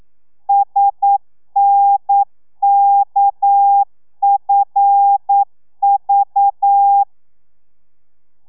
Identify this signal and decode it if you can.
Morse code, heard as a keyed tone, about 9 wpm, 800 Hz. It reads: SNKFV
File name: telegrafia / i2tfj / Lez6 40